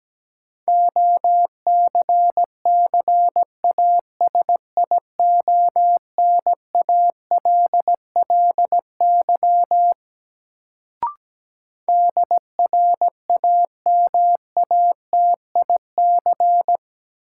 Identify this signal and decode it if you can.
Morse code, heard as a keyed tone, about 17 wpm, 700 Hz. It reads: OCCASIONALLY DRAMATIC
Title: Sentences from Top 1300 Words - New Word Drill Alt - 17wpm